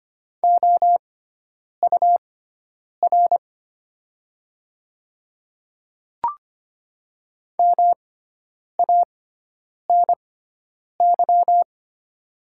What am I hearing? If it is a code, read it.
Morse code, heard as a keyed tone, about 25 wpm, spaced as for 9 wpm, 700 Hz. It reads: OUR MANY
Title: 6x Character Spacing - Top 200 Words - 25wpm